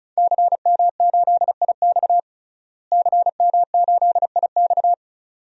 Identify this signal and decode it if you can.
Morse code, heard as a keyed tone, about 36 wpm, 700 Hz. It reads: CM8IX CM8IX
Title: DXCC-100 Call Signs - 5 Characters - Repeated 2x - 35wpm